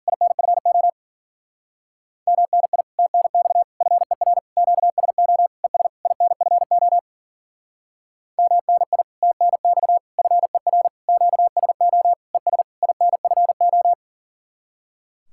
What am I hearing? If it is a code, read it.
Morse code, heard as a keyed tone, about 51 wpm, 700 Hz. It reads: INFO MNI TNX FER QSO ES INFO MNI TNX FER QSO ES INFO